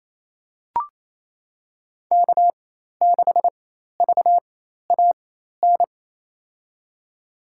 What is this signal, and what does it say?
Morse code, heard as a keyed tone, about 28 wpm, spaced as for 13 wpm, 700 Hz. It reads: K6VAN
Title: USA Call Signs - 5 Characters - 4x Character Spacing - 28wpm